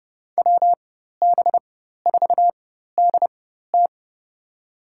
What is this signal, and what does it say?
Morse code, heard as a keyed tone, about 32 wpm, spaced as for 14 wpm, 700 Hz. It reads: WB4DT
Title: USA Call Signs - 5 Characters - 4x Character Spacing - 30wpm